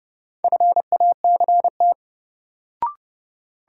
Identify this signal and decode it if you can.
Morse code, heard as a keyed tone, about 30 wpm, 700 Hz. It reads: FACT
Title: Sets of 2 Words - New Word Drill - 30wpm